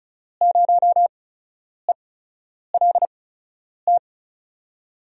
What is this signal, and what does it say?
Morse code, heard as a keyed tone, about 36 wpm, spaced as for 10 wpm, 700 Hz. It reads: 0ELT
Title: USA Call Signs - 5 Characters - 8x Character Spacing - 35wpm